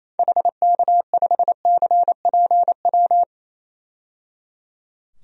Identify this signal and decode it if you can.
Morse code, heard as a keyed tone, about 28 wpm, 700 Hz. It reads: HK5CPW
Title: DXCC-100 Call Signs - 6 Characters - 28wpm